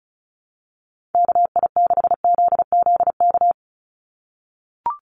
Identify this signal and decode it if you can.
Morse code, heard as a keyed tone, about 34 wpm, 700 Hz. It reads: KI6ZZK
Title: USA Call Signs 35wpm